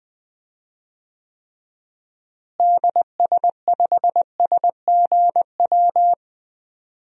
Morse code, heard as a keyed tone, about 20 wpm, 700 Hz. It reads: DS5SGW
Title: DXCC-100 Call Signs - 6 Characters - Repeated 3x - 20wpm